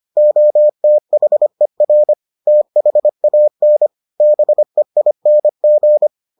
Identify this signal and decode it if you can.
Morse code, heard as a keyed tone, about 25 wpm, 600 Hz. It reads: OTHER THAN BEING